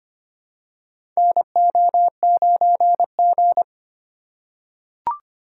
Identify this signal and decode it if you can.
Morse code, heard as a keyed tone, about 25 wpm, 700 Hz. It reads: NO9G
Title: USA Call Signs - 4 Characters - 3x Character Spacing - 25wpm